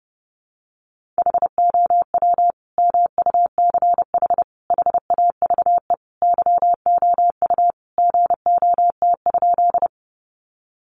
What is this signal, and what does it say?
Morse code, heard as a keyed tone, about 30 wpm, 700 Hz. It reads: HOW MUCH HAVE YOU GOT?
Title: Sentences from Top 300 Words 30wpm